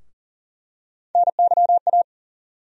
Morse code, heard as a keyed tone, about 40 wpm, 700 Hz. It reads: NYA
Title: USA Call Sign Postfixes - 3 Letters 40wpm